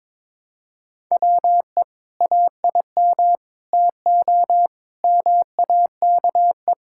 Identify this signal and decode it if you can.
Morse code, heard as a keyed tone, about 22 wpm, 700 Hz. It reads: WE AIM TO MAKE
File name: Sentences with 7-Word Phrases - 22wpm